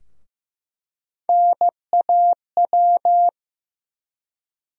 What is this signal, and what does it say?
Morse code, heard as a keyed tone, about 15 wpm, 700 Hz. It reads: NAW
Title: USA Call Sign Postfixes - 3 Letters 15wpm